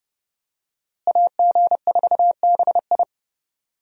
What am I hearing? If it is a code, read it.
Morse code, heard as a keyed tone, about 30 wpm, 700 Hz. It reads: AG4BI
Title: USA Call Signs - 5 Characters - Repeated 3x - 30wpm